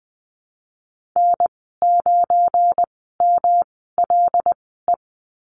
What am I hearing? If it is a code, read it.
Morse code, heard as a keyed tone, about 20 wpm, spaced as for 15 wpm, 700 Hz. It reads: N9MLE
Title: 186 Lesson 30g - Letter b - Call Signs 20wpm